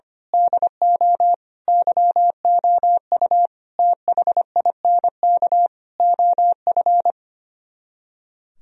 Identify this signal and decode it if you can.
Morse code, heard as a keyed tone, about 24 wpm, 700 Hz. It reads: DO YOU THINK OF
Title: Speed-Racing - Sets of 5 Words 25wpm